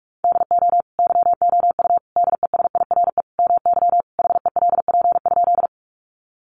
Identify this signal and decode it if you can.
Morse code, heard as a keyed tone, about 46 wpm, 700 Hz. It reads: DO YOU DESIRE MY HELP?
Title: Sentences from Top 1100 Words 45wpm